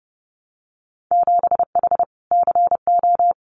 Morse code, heard as a keyed tone, about 30 wpm, 700 Hz. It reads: 7H CO
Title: Winter Field Day Exchanges Encore 30wpm